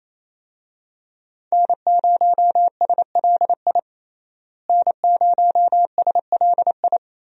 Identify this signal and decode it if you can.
Morse code, heard as a keyed tone, about 27 wpm, 700 Hz. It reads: N0SLI N0SLI
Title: USA Call Signs - 5 Characters - Repeated 2x - 28wpm